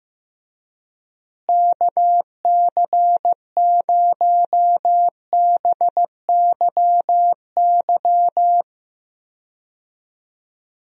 Morse code, heard as a keyed tone, about 15 wpm, 700 Hz. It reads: KC0BYY
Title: USA Call Signs - 6 Characters - Repeated 2x - 15wpm